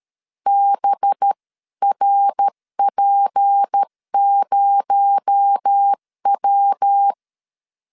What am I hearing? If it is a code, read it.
Morse code, heard as a keyed tone, about 13 wpm, 800 Hz. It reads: BRP0W
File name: LEZIONI / 7 GF9Z / 7 GF 12wpm 2122